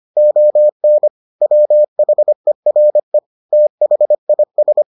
Morse code, heard as a keyed tone, about 25 wpm, 600 Hz. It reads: ON WHERE THIS